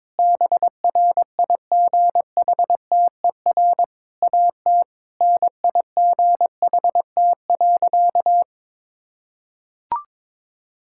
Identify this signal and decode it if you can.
Morse code, heard as a keyed tone, about 22 wpm, 700 Hz. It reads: BRIGHTER AT NIGHT.
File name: Difficult Sentences with Hints - 22wpm